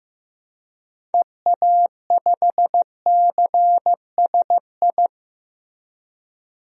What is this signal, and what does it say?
Morse code, heard as a keyed tone, about 16 wpm, 700 Hz. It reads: EA5CSI
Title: DXCC-100 Call Signs - 6 Characters - 15wpm